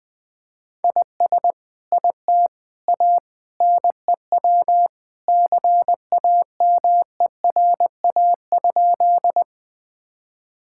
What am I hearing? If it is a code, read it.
Morse code, heard as a keyed tone, about 20 wpm, 700 Hz. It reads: IS IT A NEW CAMERA?
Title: Sentences from Top 1000 Words - Review Part 5 - 20wpm